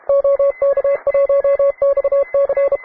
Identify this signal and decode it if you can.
Morse code, heard as a keyed tone, about 32 wpm, 560 Hz. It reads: OK1XC